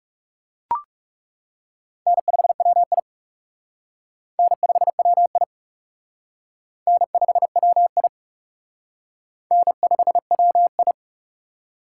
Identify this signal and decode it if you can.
Morse code, heard as a keyed tone, about 36 wpm, 700 Hz. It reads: N5WI N5WI N5WI N5WI
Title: Speed Racing - USA 1x2 Call Signs 30wpm